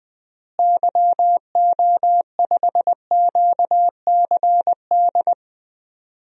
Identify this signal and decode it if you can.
Morse code, heard as a keyed tone, about 21 wpm, 700 Hz. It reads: YO5QCD
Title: DXCC-100 Call Signs - 6 Characters - Repeated 2x - 20wpm